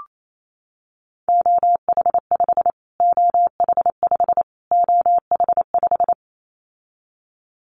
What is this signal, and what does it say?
Morse code, heard as a keyed tone, about 28 wpm, 700 Hz. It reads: OH5 OH5 OH5
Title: Callsign Prefixes for Top 25 DXCC - Repeated 3x - 28wpm